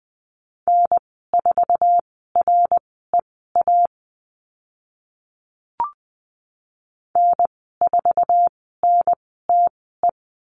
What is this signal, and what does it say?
Morse code, heard as a keyed tone, about 20 wpm, spaced as for 14 wpm, 700 Hz. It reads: N4REA N4NTE
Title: 069 Lesson 12g - Letter d - Call Signs 20wpm